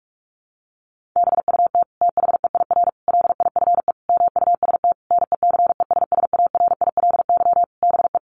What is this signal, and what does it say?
Morse code, heard as a keyed tone, about 44 wpm, 700 Hz. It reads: BUT THEIR LIFE MUST NECESSARILY BE
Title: Sentences from Top 1400 Words 45wpm